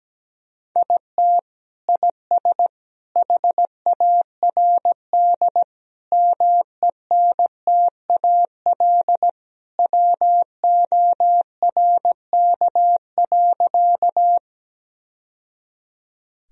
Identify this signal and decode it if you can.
Morse code, heard as a keyed tone, about 17 wpm, 700 Hz. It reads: IT IS HARD MENTAL WORK.
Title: Sentences from Top 1000 Words - Review Part 4 - 17wpm